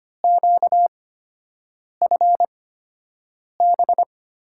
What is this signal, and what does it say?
Morse code, heard as a keyed tone, about 25 wpm, spaced as for 7 wpm, 700 Hz. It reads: QFB
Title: USA Call Signs - 5 Characters - 8x Character Spacing - 25wpm